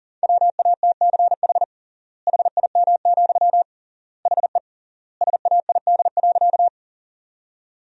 Morse code, heard as a keyed tone, about 37 wpm, 700 Hz. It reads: WATCH HIM, HE SAID.
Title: 3x Word-Spacing - Sentences from Top 300 Words - 40wpm